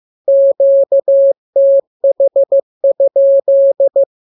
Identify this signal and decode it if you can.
Morse code, heard as a keyed tone, about 15 wpm, 550 Hz. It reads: QTH?
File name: jokes / joke 15 15 0000